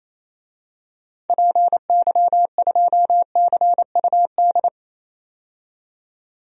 Morse code, heard as a keyed tone, about 28 wpm, 700 Hz. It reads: PY2CUD
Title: DXCC-100 Call Signs - 6 Characters - 28wpm